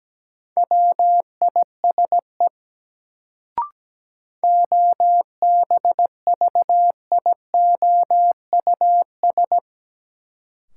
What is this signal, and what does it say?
Morse code, heard as a keyed tone, about 17 wpm, 700 Hz. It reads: WISE OBVIOUS
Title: Sentences from Top 1600 Words - New Word Drill - 17wpm